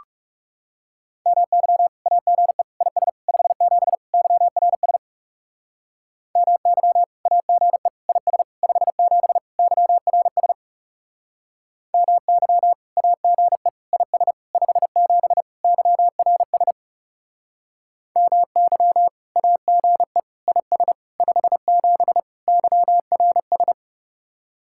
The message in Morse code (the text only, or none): MY AGE IS 57 YRS MY AGE IS 57 YRS MY AGE IS 57 YRS MY AGE IS 57 YRS